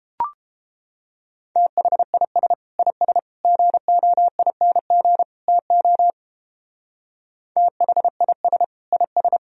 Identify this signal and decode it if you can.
Morse code, heard as a keyed tone, about 31 wpm, 700 Hz. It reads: THIS IS GOING TO THIS IS